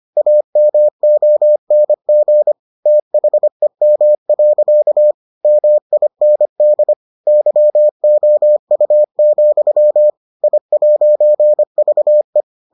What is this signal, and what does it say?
Morse code, heard as a keyed tone, about 25 wpm, 600 Hz. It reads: AMONG THEM. MIND YOU, I'VE